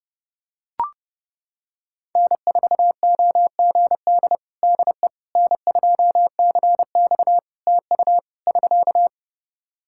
Morse code, heard as a keyed tone, about 28 wpm, 700 Hz. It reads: N4OGD DE N2CX TU <SK>
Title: Practice QSO - Exchange 4 - Ending - 30wpm